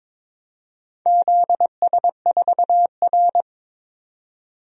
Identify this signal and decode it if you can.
Morse code, heard as a keyed tone, about 22 wpm, 700 Hz. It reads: ZS4R
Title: DXCC-100 Call Signs - 4 Characters - 22wpm